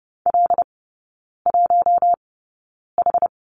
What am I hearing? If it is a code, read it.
Morse code, heard as a keyed tone, about 30 wpm, spaced as for 9 wpm, 700 Hz. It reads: L1H
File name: Lesson 17e - Letter u - Three Characters 30wpm